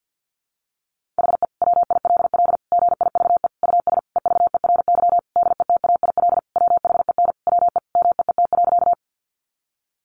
Morse code, heard as a keyed tone, about 50 wpm, 700 Hz. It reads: HE WILL GIVE US EVERY DETAIL WHEN WE MEET.
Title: Sentences from Top 1500 Words 50wpm